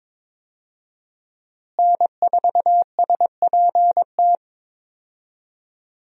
Morse code, heard as a keyed tone, about 21 wpm, 700 Hz. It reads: N4SPT
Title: USA Call Signs - 5 Characters - Repeated 2x - 22wpm